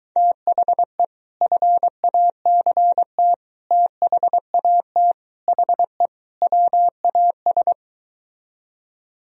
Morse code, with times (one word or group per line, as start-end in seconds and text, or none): 0.2-1.0 THE
1.4-3.3 FACT
3.7-5.1 THAT
5.5-6.1 HE
6.4-7.7 WAS